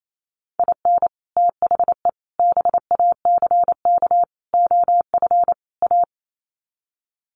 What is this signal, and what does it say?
Morse code, heard as a keyed tone, about 28 wpm, 700 Hz. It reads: IN THE BACK OF A
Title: Sets of 5 Words Encore2 28wpm